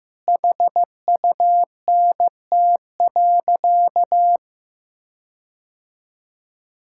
Morse code, heard as a keyed tone, about 15 wpm, 700 Hz. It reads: HUNT.
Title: Sentences from Top 1600 Words 15wpm